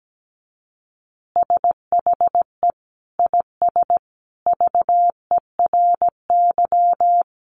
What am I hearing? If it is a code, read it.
Morse code, heard as a keyed tone, about 17 wpm, 700 Hz. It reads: SHE IS VERY
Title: Easy Phrases 17wpm